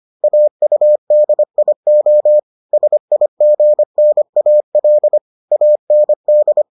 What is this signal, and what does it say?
Morse code, heard as a keyed tone, about 25 wpm, 600 Hz. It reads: AUDIO SIGNAL AND